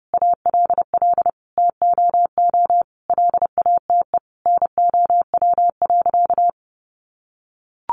Morse code, heard as a keyed tone, about 30 wpm, 700 Hz. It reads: ALL TOO LATE NOW.
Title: Sentences from Top 300 Words 30wpm